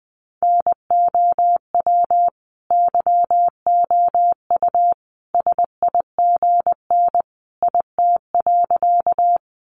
Morse code, heard as a keyed tone, about 20 wpm, 700 Hz. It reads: NOW YOU SIGN IT.
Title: Sentences from Top 1000 Words 20wpm